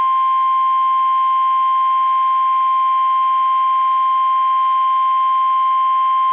Emergency Alert System (SAME) tones (the 1050 Hz attention tone). It is heard from 1.1 to 5.3 s.
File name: eas / KUSO / KUSO